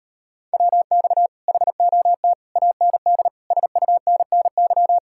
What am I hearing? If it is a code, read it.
Morse code, heard as a keyed tone, about 39 wpm, 700 Hz. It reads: WX HOT AND SUNNY